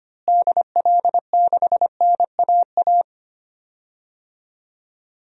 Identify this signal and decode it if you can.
Morse code, heard as a keyed tone, about 25 wpm, 700 Hz. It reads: DL6NAA
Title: DXCC-100 Call Signs - 6 Characters - Repeated 3x - 25wpm